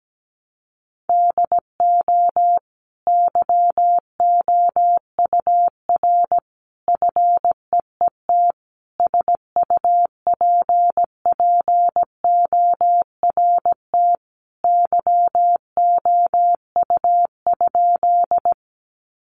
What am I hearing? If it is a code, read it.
Morse code, heard as a keyed tone, about 17 wpm, 700 Hz. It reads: DO YOUR FEET SUPPORT YOU?
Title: Sentences from Top 800 Words 17wpm